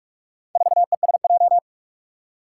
Morse code, heard as a keyed tone, about 46 wpm, 700 Hz. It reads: 4ESJ
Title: Four Challenging Letter-Number Combinations 45wpm